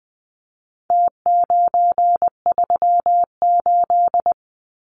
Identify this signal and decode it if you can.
Morse code, heard as a keyed tone, about 19 wpm, 700 Hz. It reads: T938
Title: Numbers - Contest Style - 4-Digit 20wpm